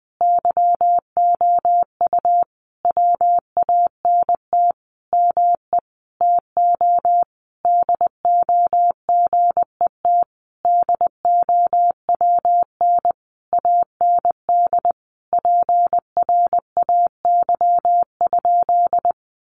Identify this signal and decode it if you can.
Morse code, heard as a keyed tone, about 19 wpm, 700 Hz. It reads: YOU WANT ME TO DO GET DOWN AND PRAY?
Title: A Sound Of Thunder 20wpm